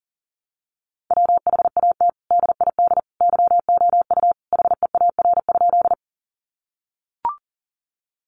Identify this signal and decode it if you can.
Morse code, heard as a keyed tone, about 40 wpm, 700 Hz. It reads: WHAT DID YOU HEAR?
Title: Sentences from Top 300 Words 40wpm